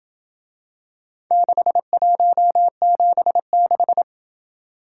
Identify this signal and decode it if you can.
Morse code, heard as a keyed tone, about 27 wpm, 700 Hz. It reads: 6176